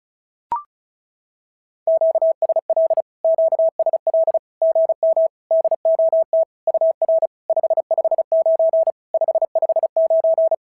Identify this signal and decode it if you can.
Morse code, heard as a keyed tone, about 36 wpm, 650 Hz. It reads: QSL QSL GM DOT UR 559 559
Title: SOTA QSO Practice 35wpm